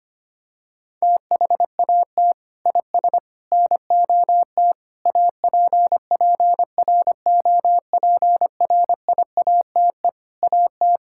Morse code, heard as a keyed tone, about 25 wpm, 700 Hz. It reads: THAT IS NOT APPROPRIATE AT
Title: Sentences from Top 1000 Words - Review Part 5 - 25wpm